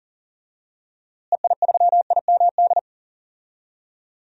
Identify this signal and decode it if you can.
Morse code, heard as a keyed tone, about 40 wpm, 700 Hz. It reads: EI3IMD
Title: DXCC-100 Call Signs - 6 Characters - Repeated 2x - 40wpm